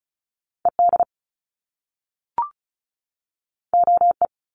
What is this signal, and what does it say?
Morse code, heard as a keyed tone, about 34 wpm, 700 Hz. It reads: ED OE
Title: Top Two Letter Combinations in Words 35wpm